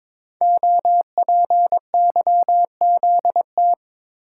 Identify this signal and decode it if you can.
Morse code, heard as a keyed tone, about 23 wpm, 700 Hz. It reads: OPYZT
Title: Code Groups - 5 Letters - 22wpm